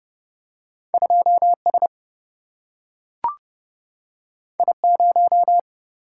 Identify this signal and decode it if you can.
Morse code, heard as a keyed tone, about 30 wpm, 700 Hz. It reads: 2S I0
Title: Single-Letter Single-Number 30wpm